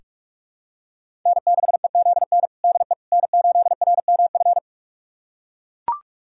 Morse code, heard as a keyed tone, about 43 wpm, 700 Hz. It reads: N6EZN DE N8RMF
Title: USA Callsign DE Callsign - 5 Characters - 45wpm